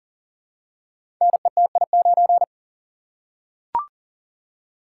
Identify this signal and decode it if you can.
Morse code, heard as a keyed tone, about 38 wpm, 700 Hz. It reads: NETI9
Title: Code Groups - 5 Letter:Numbers - 40wpm